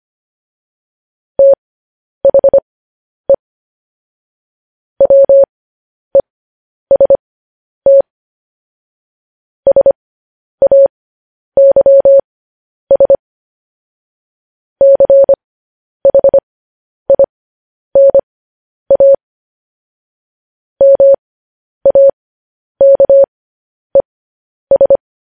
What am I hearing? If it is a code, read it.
Morse code, heard as a keyed tone, about 25 wpm, spaced as for 10 wpm, 550 Hz. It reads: THE WEST SAYS CHINA MAKES